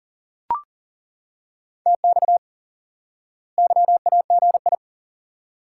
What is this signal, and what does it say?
Morse code, heard as a keyed tone, about 40 wpm, 700 Hz. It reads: TX YAGI
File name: IQE Course 3 - Lesson 89 - YAGI - Two Elements - 40wpm